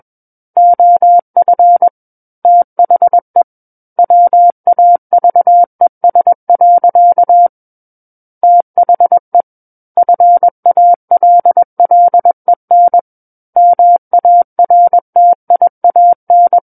Morse code, heard as a keyed tone, about 21 wpm, 700 Hz. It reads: OF THE WAVES. THE FALLEN MARTIAN
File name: Book One / 12-Chapter 12 - 21 WPM